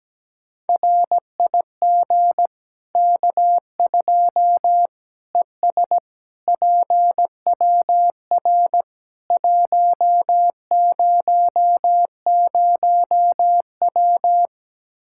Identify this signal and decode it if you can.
Morse code, heard as a keyed tone, about 17 wpm, 700 Hz. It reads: RIG K2 ES PWR 100W
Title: K1USN SST QSO Practice 17wpm